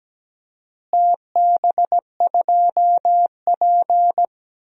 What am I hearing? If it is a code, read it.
Morse code, heard as a keyed tone, about 17 wpm, 700 Hz. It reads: TB2P
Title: DXCC-100 Call Signs - 4 Characters - Repeated 2x - 17wpm